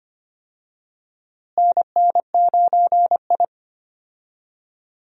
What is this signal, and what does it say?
Morse code, heard as a keyed tone, about 25 wpm, 700 Hz. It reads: NN9I
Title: USA Call Signs - 4 Characters - Repeated 2x - 25wpm